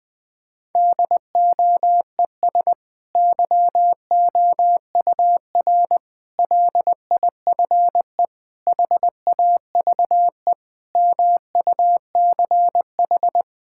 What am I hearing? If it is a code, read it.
Morse code, heard as a keyed tone, about 20 wpm, 700 Hz. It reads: DOES YOUR LIFE HAVE MUCH
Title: Sentences from Top 1000 Words - Review Part 6 - 20wpm